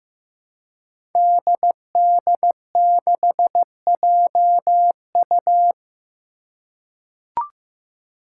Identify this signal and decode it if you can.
Morse code, heard as a keyed tone, about 16 wpm, 700 Hz. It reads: DD6JU
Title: Five Challenging Letter-Number Combinations 15wpm